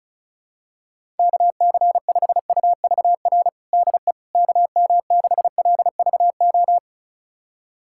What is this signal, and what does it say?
Morse code, heard as a keyed tone, about 34 wpm, 700 Hz. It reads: KC5UVR DE KM6LVO
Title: USA Callsign DE Callsign 35wpm